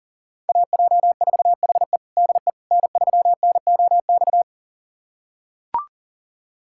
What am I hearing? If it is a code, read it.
Morse code, heard as a keyed tone, about 40 wpm, 700 Hz. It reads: AJ4HE DE N3NOX
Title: USA Callsign DE Callsign - 5 Characters - 40wpm